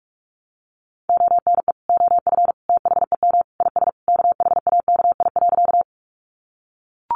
Morse code, heard as a keyed tone, about 44 wpm, 700 Hz. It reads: ONE OF THEM IS KHAKI.
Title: Additional Easy Phrases 45wpm